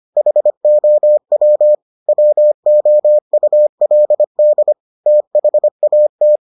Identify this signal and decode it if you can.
Morse code, heard as a keyed tone, about 25 wpm, 600 Hz. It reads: HOW WOULD THAT